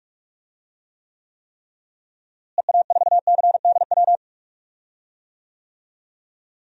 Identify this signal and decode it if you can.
Morse code, heard as a keyed tone, about 45 wpm, 700 Hz. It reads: EA4CDW
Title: DXCC-100 Call Signs - 6 Characters - Repeated 3x - 45wpm